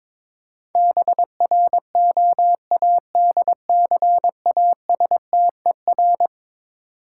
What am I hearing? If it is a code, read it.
Morse code, heard as a keyed tone, about 22 wpm, 700 Hz. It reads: BROADCASTER
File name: practice / 22wpm / Eleven Letter Words - 22wpm